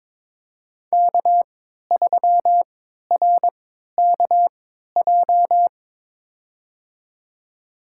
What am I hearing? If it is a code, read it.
Morse code, heard as a keyed tone, about 22 wpm, spaced as for 13 wpm, 700 Hz. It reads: K3RKJ
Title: USA Call Signs - 5 Characters - 3x Character Spacing - 22wpm